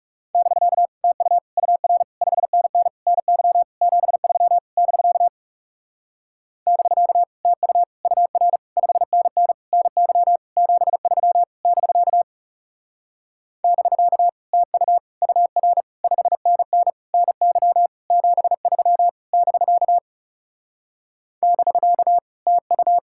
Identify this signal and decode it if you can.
Morse code, heard as a keyed tone, about 37 wpm, 700 Hz. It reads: <BK> TU UR 5NN NY 73 <BK> <BK> TU UR 5NN NY 73 <BK> <BK> TU UR 5NN NY 73 <BK> <BK> TU
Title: Speed-Racing Common QSO Phrases - 30wpm